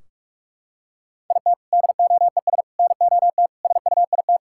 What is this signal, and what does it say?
Morse code, heard as a keyed tone, about 46 wpm, 700 Hz. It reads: IT DOES NOT SUIT
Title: Sentences from Top 1600 Words 45wpm